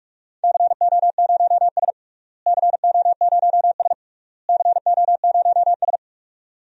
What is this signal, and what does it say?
Morse code, heard as a keyed tone, about 46 wpm, 700 Hz. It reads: CO0S CO0S CO0S
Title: DXCC-100 Call Signs - 4 Characters - Repeated 3x - 45wpm